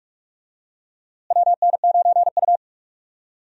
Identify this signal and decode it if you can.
Morse code, heard as a keyed tone, about 42 wpm, 700 Hz. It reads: WN9U
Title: USA Call Signs - 4 Characters - Repeated 2x - 45wpm